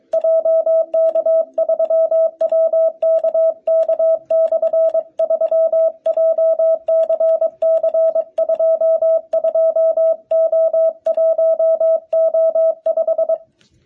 Morse code, heard as a keyed tone, about 23 wpm, 650 Hz. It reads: JK3WKK/3 JCC22O1O5
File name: jh8cbh / 2021cw